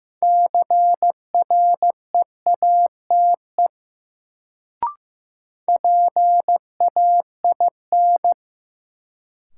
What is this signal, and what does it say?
Morse code, heard as a keyed tone, about 15 wpm, 700 Hz. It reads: CREATE PAIN
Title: Sentences from Top 1600 Words - New Word Drill Alt - 15wpm